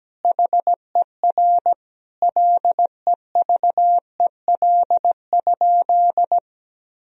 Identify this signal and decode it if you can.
Morse code, heard as a keyed tone, about 17 wpm, 700 Hz. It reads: HER LEVEL?
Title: Sentences from Top 1000 Words - Review Part 4 - 17wpm